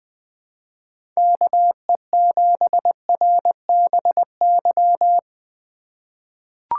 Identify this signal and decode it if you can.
Morse code, heard as a keyed tone, about 20 wpm, 700 Hz. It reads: KE7RBY
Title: USA Call Signs - 6 Characters - 2x Character Spacing - 20wpm